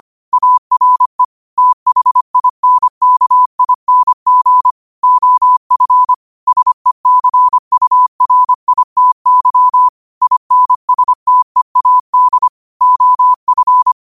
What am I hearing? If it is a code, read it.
Morse code, heard as a keyed tone, about 25 wpm, 1000 Hz. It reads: ARE THINKING OF SECURITY INSTEAD OF
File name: qotdfiles / 25wpm